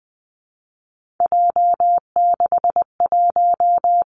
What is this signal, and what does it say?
Morse code, heard as a keyed tone, about 20 wpm, 700 Hz. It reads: J61
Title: Three Challenging Letter-Number Combinations 20wpm